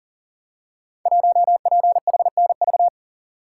Morse code, heard as a keyed tone, about 38 wpm, 700 Hz. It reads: 1PHNV
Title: Code Groups - 5 Letter:Numbers - 40wpm